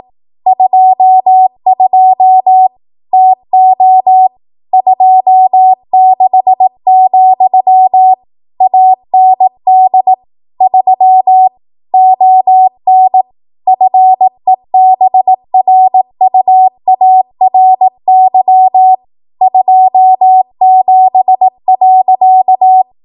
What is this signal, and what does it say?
Morse code, heard as a keyed tone, about 18 wpm, 750 Hz. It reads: 22 TO 26, AND 3 ON FEBRUARY 27.